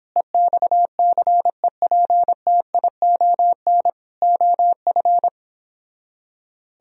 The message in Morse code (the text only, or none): EXCEPTION OF